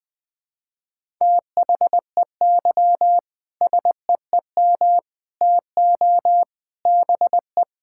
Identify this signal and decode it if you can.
Morse code, heard as a keyed tone, about 20 wpm, 700 Hz. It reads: THEY SEEM TO BE
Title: Sentences with 3-Word Phrases - 20wpm